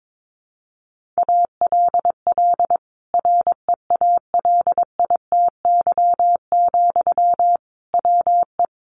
Morse code, heard as a keyed tone, about 22 wpm, 700 Hz. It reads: ALL REALITY, WE
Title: Sentences from Top 1400 Words 22wpm